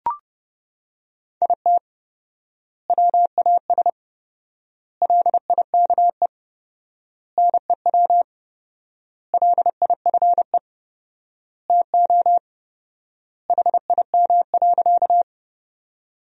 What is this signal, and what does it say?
Morse code, heard as a keyed tone, about 32 wpm, 700 Hz. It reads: IT WAS LIKE NEW LIFE TO HIM.
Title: 4x Word-Spacing - Sentences from Top 300 Words - 30wpm